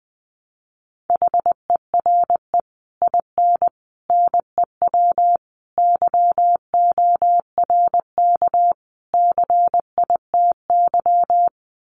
Morse code, heard as a keyed tone, about 20 wpm, 700 Hz. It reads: HERE IN NEW YORK CITY
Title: Sets of 5 Words Encore2 20wpm